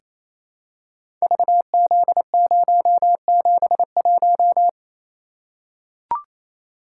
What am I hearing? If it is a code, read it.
Morse code, heard as a keyed tone, about 29 wpm, 700 Hz. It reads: VZ071
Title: Code Groups - 5 Letter:Numbers - 28wpm